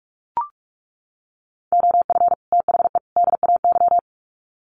Morse code, heard as a keyed tone, about 44 wpm, 700 Hz. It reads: OF THE DAY
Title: Sets of 3 Words 45wpm